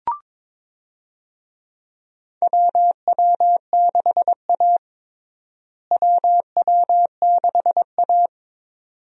Morse code, heard as a keyed tone, about 22 wpm, 700 Hz. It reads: WW6A WW6A
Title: USA Call Signs - 4 Characters - Repeated 2x - 22wpm